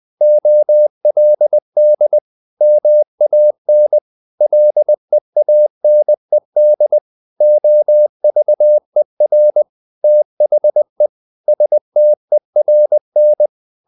Morse code, heard as a keyed tone, about 20 wpm, 600 Hz. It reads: OLD MAN LEANED OVER THE STERN
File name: the old man and the sea part 2 20wpm